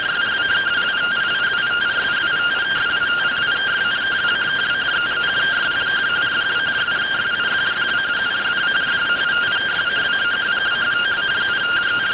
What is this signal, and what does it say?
Two-tone teleprinter signal, about 170 Hz shift.